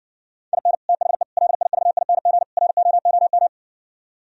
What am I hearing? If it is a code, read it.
Morse code, heard as a keyed tone, about 58 wpm, 700 Hz. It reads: IN THE LIVING ROOM